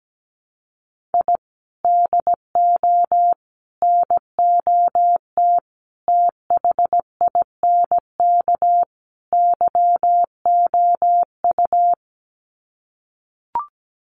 Morse code, heard as a keyed tone, about 17 wpm, 700 Hz. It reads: I DO NOT THINK YOU
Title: Sets of 5 Words 17wpm